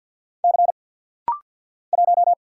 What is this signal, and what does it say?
Morse code, heard as a keyed tone, about 48 wpm, 690 Hz. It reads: C 1
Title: Single Letter-Number Alt - Rapid-Fire 50wpm